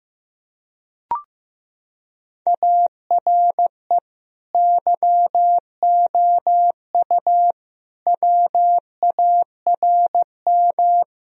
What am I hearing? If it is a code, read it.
Morse code, heard as a keyed tone, about 15 wpm, 700 Hz. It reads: ARE YOU WARM
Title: Sentences from Top 1000 Words - Review Part 2 - 15wpm